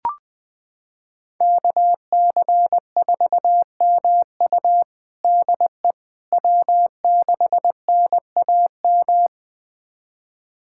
Morse code, heard as a keyed tone, about 21 wpm, 700 Hz. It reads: KC4MU DE W6NAM
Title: USA Callsign DE Callsign - 5 Characters - 20wpm